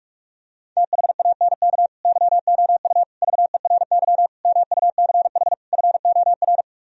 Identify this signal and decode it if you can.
Morse code, heard as a keyed tone, about 49 wpm, 700 Hz. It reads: THANK YOU VERY MUCH FOR